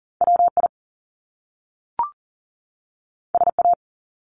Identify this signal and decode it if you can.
Morse code, heard as a keyed tone, about 41 wpm, 700 Hz. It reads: WI SA
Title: Top Two Letter Combinations in Words 40wpm